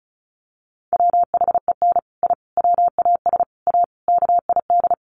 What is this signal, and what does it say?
Morse code, heard as a keyed tone, about 34 wpm, 700 Hz. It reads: WHEN I WAS A KID
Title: Sets of 5 Words 35wpm